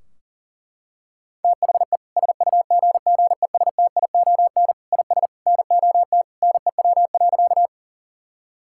Morse code, heard as a keyed tone, about 40 wpm, 700 Hz. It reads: THE SUGGESTION IS NOT NEW.
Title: Sentences from Top 1600 Words 40wpm